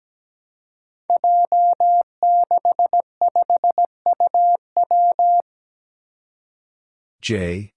Morse code, heard as a keyed tone, about 17 wpm, 700 Hz. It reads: J65UW
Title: Five Challenging Letter-Number Combinations 17wpm